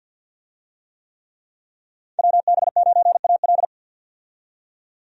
Morse code, heard as a keyed tone, about 47 wpm, 700 Hz. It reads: WB9AL
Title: USA Call Signs - 5 Characters - 50wpm